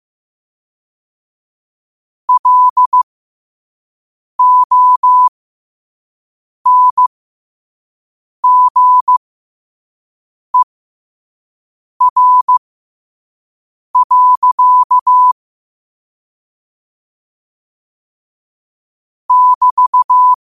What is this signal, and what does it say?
Morse code, heard as a keyed tone, about 15 wpm, spaced as for 5 wpm, 1000 Hz. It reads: LONGER. <BT>
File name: qotdfiles / 5wpm